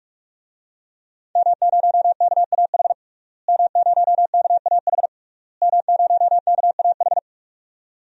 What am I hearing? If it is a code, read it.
Morse code, heard as a keyed tone, about 41 wpm, 700 Hz. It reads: M0KAH M0KAH M0KAH
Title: DXCC-100 Call Signs - 5 Characters - Repeated 3x - 45wpm